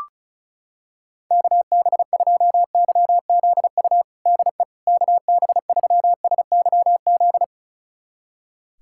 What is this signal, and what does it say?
Morse code, heard as a keyed tone, about 36 wpm, 700 Hz. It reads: KB2YZU DE KB3SYZ
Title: USA Callsign DE Callsign 35wpm